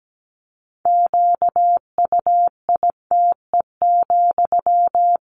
Morse code, heard as a keyed tone, about 17 wpm, 700 Hz. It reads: QUITE,
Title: Sentences from Top 1500 Words 17wpm